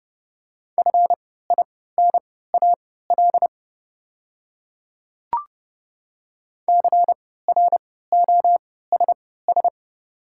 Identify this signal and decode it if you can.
Morse code, heard as a keyed tone, about 30 wpm, spaced as for 17 wpm, 700 Hz. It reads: FINAL CROSS
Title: 3x Character Spacing - Top 500 Words - 30wpm